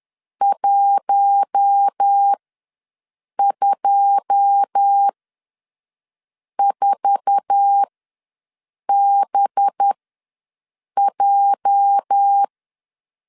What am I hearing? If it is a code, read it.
Morse code, heard as a keyed tone, about 10 wpm, spaced as for 5 wpm, 800 Hz. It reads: 124BJ
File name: LEZIONI / 9 2 ws 10wpm 4130